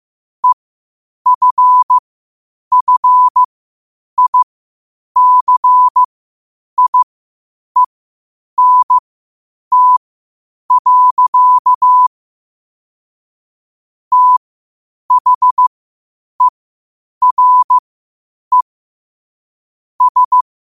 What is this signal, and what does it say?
Morse code, heard as a keyed tone, about 15 wpm, spaced as for 8 wpm, 1000 Hz. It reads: EFFICIENT. THERE S
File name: qotdfiles / 8wpm